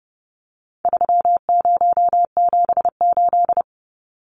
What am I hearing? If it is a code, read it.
Morse code, heard as a keyed tone, about 30 wpm, 700 Hz. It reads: 3078
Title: Numbers - 4-Digit 30wpm